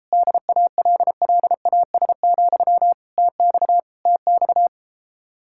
Morse code, heard as a keyed tone, about 33 wpm, 700 Hz. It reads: DALLAS, TX TX